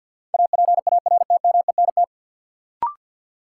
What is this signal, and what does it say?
Morse code, heard as a keyed tone, about 48 wpm, 700 Hz. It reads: APARTMENT
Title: Sentences from Top 900 Words - New Word Drill - 50wpm